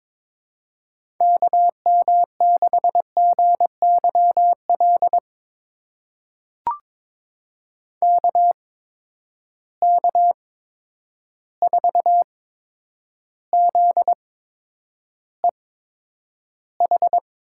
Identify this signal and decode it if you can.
Morse code, heard as a keyed tone, about 22 wpm, spaced as for 9 wpm, 700 Hz. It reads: KM6GYL KK4ZEH
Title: USA Call Signs - 6 Characters - 8x Character Spacing - 22wpm